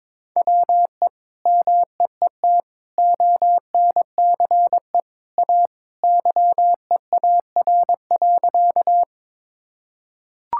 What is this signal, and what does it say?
Morse code, heard as a keyed tone, about 22 wpm, 700 Hz. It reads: WE MEET ONCE A YEAR.
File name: practice / 22wpm / Sentences with 2-Word Phrases - 22wpm